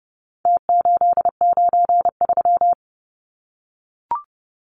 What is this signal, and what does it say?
Morse code, heard as a keyed tone, about 30 wpm, 700 Hz. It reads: T893
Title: Numbers - Contest Style - 4-Digit 30wpm